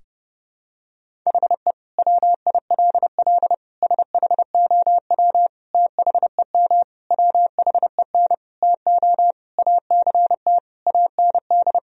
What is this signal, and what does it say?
Morse code, heard as a keyed tone, about 30 wpm, 700 Hz. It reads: HE WILL SHOW THEM WHEN TO ACT AND
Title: Sentences from Top 1000 Words - Review Part 1 - 30wpm